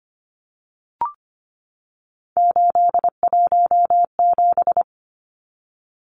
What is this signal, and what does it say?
Morse code, heard as a keyed tone, about 25 wpm, 700 Hz. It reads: 817
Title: Numbers - 3-Digit 25wpm